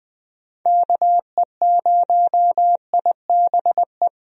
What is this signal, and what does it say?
Morse code, heard as a keyed tone, about 20 wpm, 700 Hz. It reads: KE0IBE
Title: USA Call Signs - 6 Characters - Repeated 3x - 20wpm